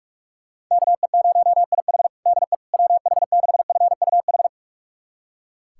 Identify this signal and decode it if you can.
Morse code, heard as a keyed tone, about 41 wpm, 690 Hz. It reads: KE0IH DE WH6FUH
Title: USA Callsign DE Callsign 45wpm